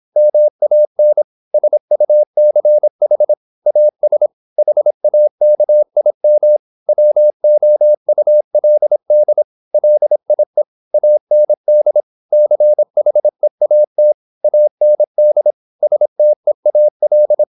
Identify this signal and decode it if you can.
Morse code, heard as a keyed tone, about 26 wpm, 600 Hz. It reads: MAN SUCH AS HAKIM WOULD LIE AND CHEAT AND STEAL
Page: Never 26wpm | CW med Gnister